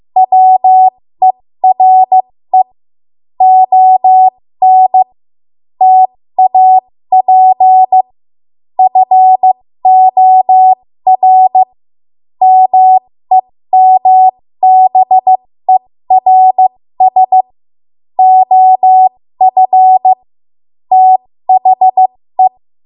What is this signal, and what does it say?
Morse code, heard as a keyed tone, about 15 wpm, spaced as for 13 wpm, 750 Hz. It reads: WERE ON TAP FOR MEMBERS OF THE
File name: Morse / 13WPM